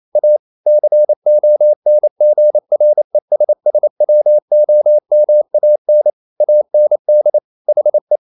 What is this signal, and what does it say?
Morse code, heard as a keyed tone, about 28 wpm, 600 Hz. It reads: A CONGRESSWOMAN AND HE
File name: Never 0027wpm